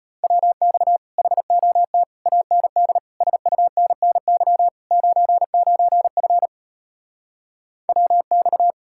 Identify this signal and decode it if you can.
Morse code, heard as a keyed tone, about 33 wpm, 700 Hz. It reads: WX HOT AND SUNNY 99F WX